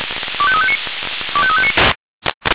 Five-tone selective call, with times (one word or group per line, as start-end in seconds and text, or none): 0.4-0.8 25359
1.3-1.7 25359